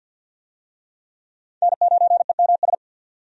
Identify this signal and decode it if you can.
Morse code, heard as a keyed tone, about 52 wpm, 700 Hz. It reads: N9EMS
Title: USA Call Signs - 5 Characters - 50wpm